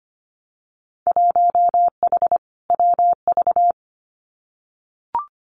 Morse code, heard as a keyed tone, about 25 wpm, 700 Hz. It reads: 1H WV
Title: Winter Field Day Exchanges 25wpm